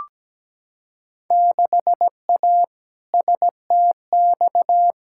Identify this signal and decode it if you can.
Morse code, heard as a keyed tone, about 17 wpm, 700 Hz. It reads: 6A STX
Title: Field Day Exchanges - 17wpm